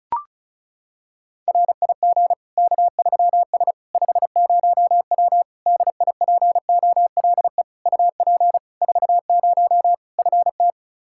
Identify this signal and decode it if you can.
Morse code, heard as a keyed tone, about 34 wpm, 700 Hz. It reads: RIG K3S 50W DIPOLE UP 40 FT
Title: K1USN SST QSO Practice 35wpm